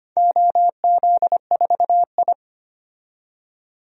Morse code, heard as a keyed tone, about 25 wpm, 700 Hz. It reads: OZ4I
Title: DXCC-100 Call Signs - 4 Characters - 25wpm